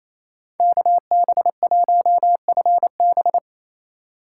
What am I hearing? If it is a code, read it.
Morse code, heard as a keyed tone, about 28 wpm, 700 Hz. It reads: KB1FB
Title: USA Call Signs - 5 Characters - 4x Character Spacing - 28wpm